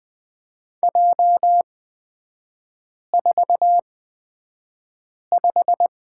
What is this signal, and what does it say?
Morse code, heard as a keyed tone, about 20 wpm, spaced as for 5 wpm, 700 Hz. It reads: J45
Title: 243 Lesson 39e - Letter z - Three Characters 20wpm